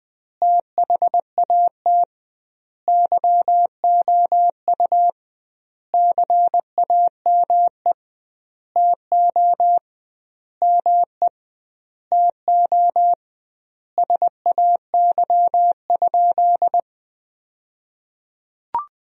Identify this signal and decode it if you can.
Morse code, heard as a keyed tone, about 21 wpm, 700 Hz. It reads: THAT YOU CAME TO ME TO SAY?
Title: 2x Word-Spacing - Sentences from Top 200 Words - 20wpm